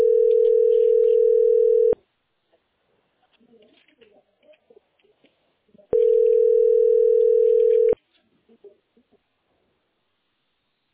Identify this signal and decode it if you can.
Ringback tone (North America), 440 + 480 Hz.